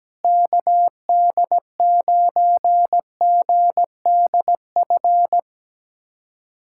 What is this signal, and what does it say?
Morse code, heard as a keyed tone, about 17 wpm, 700 Hz. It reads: KD9GDF
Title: USA Callsign DE Callsign 17wpm